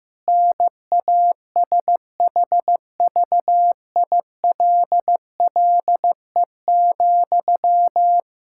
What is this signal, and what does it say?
Morse code, heard as a keyed tone, about 15 wpm, 700 Hz. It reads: NASHVILLE,
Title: Practice QSO - Exchange 1 - Beginning - 15wpm